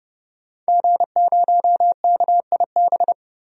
Morse code, heard as a keyed tone, about 30 wpm, 700 Hz. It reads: G0KIB
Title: DXCC-100 Call Signs - 5 Characters - 30wpm